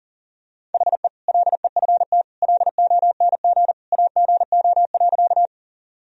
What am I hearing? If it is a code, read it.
Morse code, heard as a keyed tone, about 40 wpm, 700 Hz. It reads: HE LEFT LONG AGO.
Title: Sentences from Top 1000 Words - Review Part 3 - 40wpm